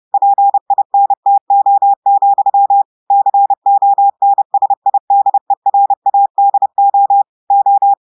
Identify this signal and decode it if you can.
Morse code, heard as a keyed tone, about 30 wpm, 800 Hz. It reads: PINTO, CONSIDERADO O
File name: audios / portugais / portugais texte en clair 1 vitesse 30 mots minutes